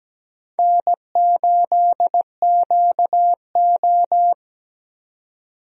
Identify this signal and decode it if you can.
Morse code, heard as a keyed tone, about 17 wpm, 700 Hz. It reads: N8QO
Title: USA Call Signs - 4 Characters - 17wpm